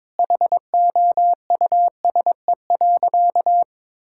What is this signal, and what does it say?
Morse code, heard as a keyed tone, about 22 wpm, 700 Hz. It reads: HOUSE.
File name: Difficult Sentences with Hints - 22wpm